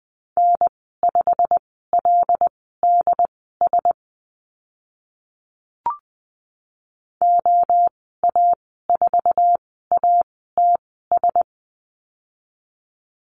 Morse code, heard as a keyed tone, about 20 wpm, spaced as for 14 wpm, 700 Hz. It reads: N5LDS OA4ATS
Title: 088 Lesson 15g - Number 5 - Call Signs 20wpm